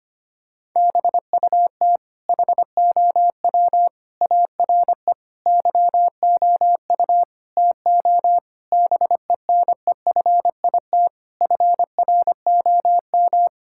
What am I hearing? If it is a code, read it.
Morse code, heard as a keyed tone, about 25 wpm, 700 Hz. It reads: BUT HOW ARE YOU TO BENEFIT FROM
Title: Sentences from Top 1000 Words - Review Part 6 - 25wpm